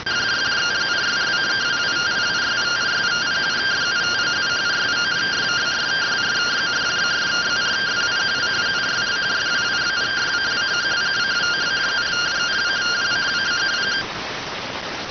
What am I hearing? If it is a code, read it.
Two-tone teleprinter signal, about 170 Hz shift.